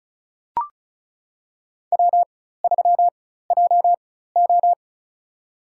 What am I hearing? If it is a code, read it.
Morse code, heard as a keyed tone, about 36 wpm, spaced as for 16 wpm, 700 Hz. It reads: W3JO
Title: USA Call Signs - 4 Characters - 4x Character Spacing - 35wpm